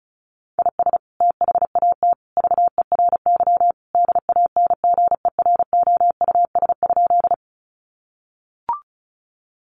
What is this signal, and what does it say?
Morse code, heard as a keyed tone, about 34 wpm, 700 Hz. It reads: IS THAT VERY DANGEROUS?
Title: Sentences from Top 1200 Words 35wpm